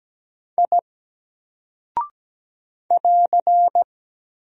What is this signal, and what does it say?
Morse code, heard as a keyed tone, about 17 wpm, 700 Hz. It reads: I <AR>
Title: Single Character (Letter:Number:Punctuation:Prosign) 17wpm